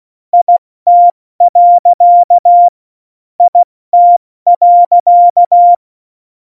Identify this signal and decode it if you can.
Morse code, heard as a keyed tone, about 16 wpm, 700 Hz. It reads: IT. IT.
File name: Morsekurs / Witze / Witz 25 Juli 17, Tempo 70 F80